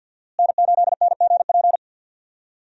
Morse code, heard as a keyed tone, about 53 wpm, 700 Hz. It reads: N8NGP
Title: USA Call Signs - 5 Characters - Repeated 2x - 50wpm